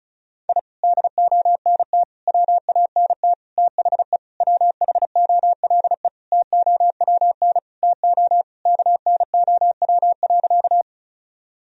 Morse code, heard as a keyed tone, about 36 wpm, 700 Hz. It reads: I DONT WANT THE WHOLE TOWN TO KNOW.
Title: Sentences from Top 1000 Words - Review Part 2 - 35wpm